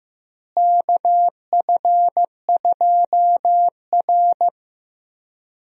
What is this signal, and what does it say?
Morse code, heard as a keyed tone, about 15 wpm, 700 Hz. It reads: KF2R
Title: USA Call Signs - 4 Characters - Repeated 2x - 15wpm